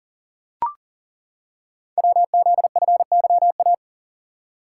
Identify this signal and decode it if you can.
Morse code, heard as a keyed tone, about 40 wpm, 700 Hz. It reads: WZFYA
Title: Code Groups - 5 Letters - 40wpm